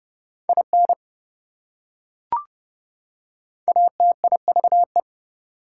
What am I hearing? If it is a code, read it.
Morse code, heard as a keyed tone, about 28 wpm, 700 Hz. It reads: IN ATIVE